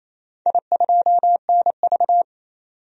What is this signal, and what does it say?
Morse code, heard as a keyed tone, about 27 wpm, 700 Hz. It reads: I2NV
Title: DXCC-100 Call Signs - 4 Characters - 28wpm